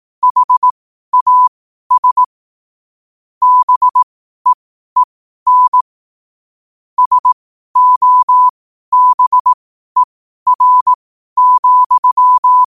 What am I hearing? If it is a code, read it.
Morse code, heard as a keyed tone, about 18 wpm, spaced as for 12 wpm, 1000 Hz. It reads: HAS BEEN SOBER,
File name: qotdfiles / 12wpm